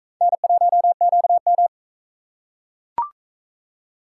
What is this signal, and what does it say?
Morse code, heard as a keyed tone, about 43 wpm, 700 Hz. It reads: N1QM